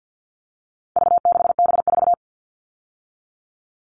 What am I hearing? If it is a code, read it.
Morse code, heard as a keyed tone, about 50 wpm, 700 Hz. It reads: V6B4
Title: All Cut Numbers - 4-Digit 50wpm